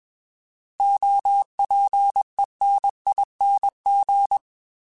Morse code, heard as a keyed tone, about 21 wpm, 800 Hz. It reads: OPENING